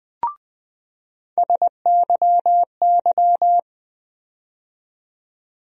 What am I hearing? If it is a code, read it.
Morse code, heard as a keyed tone, about 20 wpm, 700 Hz. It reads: SYY
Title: USA Call Sign Postfixes - 3 Letters 20wpm